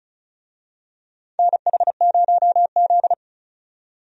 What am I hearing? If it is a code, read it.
Morse code, heard as a keyed tone, about 36 wpm, 700 Hz. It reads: NH0Z
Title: USA Call Signs - 4 Characters - 35wpm